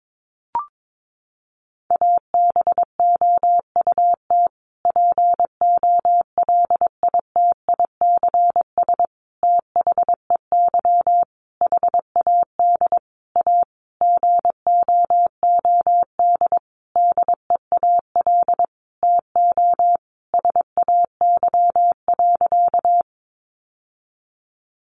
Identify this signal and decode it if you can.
Morse code, heard as a keyed tone, about 22 wpm, 700 Hz. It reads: ABOUT POLITICS THEY HAD A GOOD DEAL TO SAY.
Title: Sentences from Top 1000 Words 22wpm